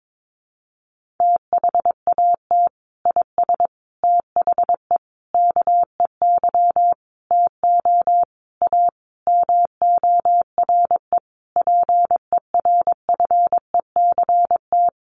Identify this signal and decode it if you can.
Morse code, heard as a keyed tone, about 22 wpm, 700 Hz. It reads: THAT IS THE KEY TO A MORE PERFECT
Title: Sentences from Top 1400 Words 22wpm